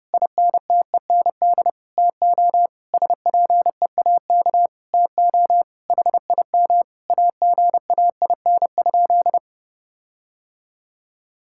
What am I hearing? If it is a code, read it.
Morse code, heard as a keyed tone, about 30 wpm, 700 Hz. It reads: INTEND TO SPEAK TO HIM AGAIN?
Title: Sentences from Top 1600 Words 30wpm